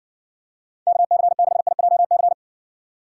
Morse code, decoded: KC6IJC

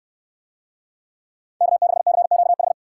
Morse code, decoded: K6CQH

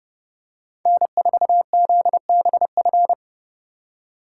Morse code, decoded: N4ZBF

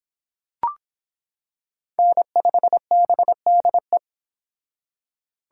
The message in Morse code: N5BDE

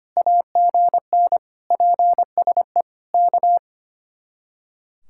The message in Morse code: AGN PSE K

